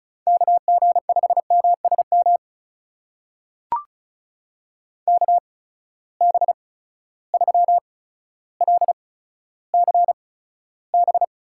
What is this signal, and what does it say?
Morse code, heard as a keyed tone, about 36 wpm, spaced as for 15 wpm, 700 Hz. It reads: KG5MSM KB3LCB
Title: USA Call Signs - 6 Characters - 8x Character Spacing - 35wpm